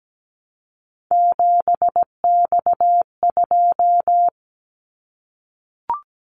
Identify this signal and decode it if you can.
Morse code, heard as a keyed tone, about 17 wpm, 700 Hz. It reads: 7X2
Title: Callsign Prefixes for Top 100 DXCC - 17wpm